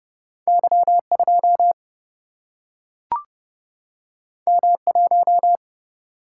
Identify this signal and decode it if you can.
Morse code, heard as a keyed tone, about 29 wpm, 700 Hz. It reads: Y 2 M 1
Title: Single-Letter Single-Number 30wpm